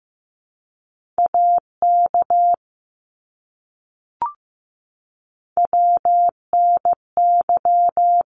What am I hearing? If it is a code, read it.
Morse code, heard as a keyed tone, about 15 wpm, 700 Hz. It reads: AK WNY
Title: ARRL Sections 15wpm